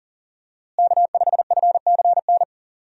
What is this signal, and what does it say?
Morse code, heard as a keyed tone, about 37 wpm, 700 Hz. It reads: K5FCN
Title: USA Call Signs - 5 Characters - Repeated 3x - 40wpm